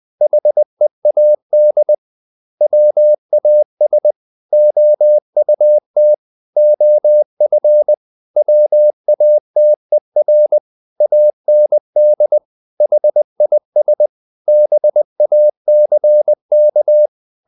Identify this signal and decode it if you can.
Morse code, heard as a keyed tone, about 20 wpm, 600 Hz. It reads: HEAD WAS OUT OF WATER AND HIS BACK
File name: the old man and the sea part 2 20wpm